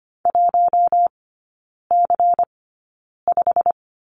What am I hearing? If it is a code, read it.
Morse code, heard as a keyed tone, about 25 wpm, spaced as for 9 wpm, 700 Hz. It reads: 1C5
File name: course / 25:10wpm / 093 Lesson 16e - Letter c - Three Characters 25wpm